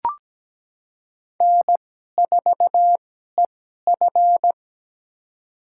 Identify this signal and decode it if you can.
Morse code, heard as a keyed tone, about 17 wpm, spaced as for 12 wpm, 700 Hz. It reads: N4EF
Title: USA Call Signs - 4 Characters - 2x Character Spacing - 17wpm